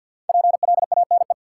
Morse code, heard as a keyed tone, about 52 wpm, 700 Hz. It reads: PLANE
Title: Sentences from Top 1000 Words - Review Part 3 - New Word Drill - 50wpm